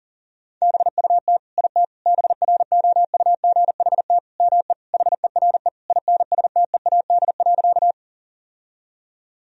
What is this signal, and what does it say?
Morse code, heard as a keyed tone, about 40 wpm, 700 Hz. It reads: BUT IT BROUGHT ME HERE INSTEAD.
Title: Sentences from Top 1000 Words - Review Part 4 - 40wpm